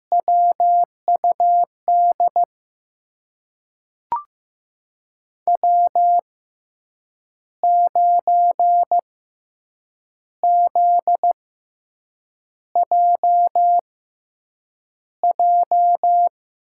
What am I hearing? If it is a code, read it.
Morse code, heard as a keyed tone, about 16 wpm, spaced as for 7 wpm, 700 Hz. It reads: WUD W9ZJJ
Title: USA Call Signs - 5 Characters - 6x Character Spacing - 15wpm